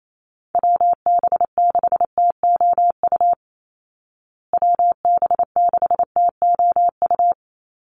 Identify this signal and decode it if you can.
Morse code, heard as a keyed tone, about 28 wpm, 700 Hz. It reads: WB6TOU WB6TOU
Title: Call Signs Repeated 2x - 28wpm